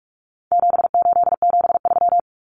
Morse code, heard as a keyed tone, about 44 wpm, 700 Hz. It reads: 7873
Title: Common Cut Numbers - 4-Digit 45wpm